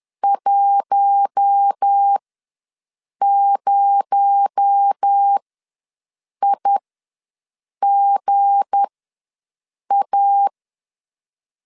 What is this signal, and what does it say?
Morse code, heard as a keyed tone, about 10 wpm, spaced as for 6 wpm, 800 Hz. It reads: 10IGA
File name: LEZIONI / 9 286 ws 10wpm 4032